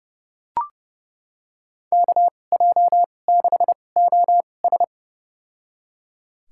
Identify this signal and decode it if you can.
Morse code, heard as a keyed tone, about 30 wpm, spaced as for 22 wpm, 700 Hz. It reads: KJ6OS
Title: USA Call Signs - 5 Characters - 2x Character Spacing - 30wpm